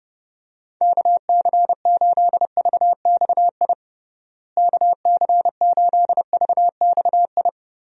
Morse code, heard as a keyed tone, about 30 wpm, 700 Hz. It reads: KC8VXI KC8VXI
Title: USA Call Signs - 6 Characters - Repeated 3x - 30wpm